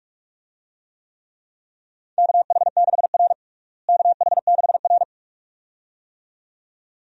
Morse code, heard as a keyed tone, about 46 wpm, 700 Hz. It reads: KH6R KH6R
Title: DXCC-100 Call Signs - 4 Characters - Repeated 2x - 45wpm